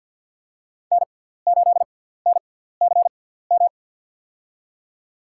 Morse code, heard as a keyed tone, about 50 wpm, spaced as for 17 wpm, 700 Hz. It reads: N8NCM
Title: USA Call Signs - 5 Characters - 6x Character Spacing - 50wpm